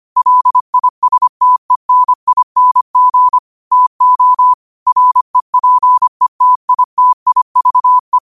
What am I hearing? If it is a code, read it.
Morse code, heard as a keyed tone, about 25 wpm, 1000 Hz. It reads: LISTENING TO REPETITIVE
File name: qotdfiles / 25wpm